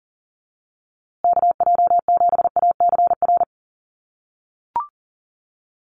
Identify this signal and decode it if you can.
Morse code, heard as a keyed tone, about 40 wpm, 700 Hz. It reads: KJ7ACR
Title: USA Call Signs Repeated 2x 40wpm